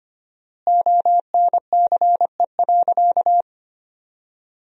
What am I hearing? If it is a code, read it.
Morse code, heard as a keyed tone, about 25 wpm, 700 Hz. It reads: ONCE.
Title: 4x Word-Spacing - Sentences from Top 400 Words - 25wpm